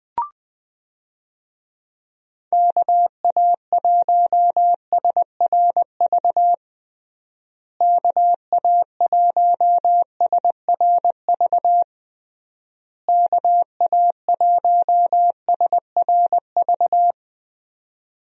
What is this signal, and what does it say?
Morse code, heard as a keyed tone, about 20 wpm, 700 Hz. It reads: KA1SRV KA1SRV KA1SRV
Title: USA Call Signs - 6 Characters - Repeated 3x - 20wpm